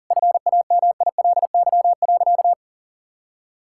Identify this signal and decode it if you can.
Morse code, heard as a keyed tone, about 37 wpm, 700 Hz. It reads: FAMILY.
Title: Sentences from Top 1000 Words - Review Part 6 - 40wpm